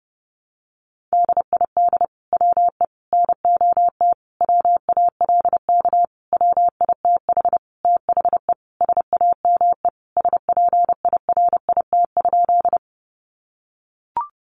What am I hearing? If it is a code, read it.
Morse code, heard as a keyed tone, about 30 wpm, 700 Hz. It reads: DID WE NOT WALK WITH THE SAME SPIRIT?
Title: Sentences from Top 1400 Words 30wpm